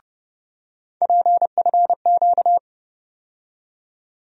Morse code, heard as a keyed tone, about 30 wpm, 700 Hz. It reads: PFQ
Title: USA Call Sign Postfixes - 3 Letters 30wpm